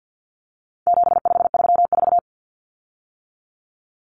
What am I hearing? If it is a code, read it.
Morse code, heard as a keyed tone, about 50 wpm, 700 Hz. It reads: 7534
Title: Common Cut Numbers - 4-Digit 50wpm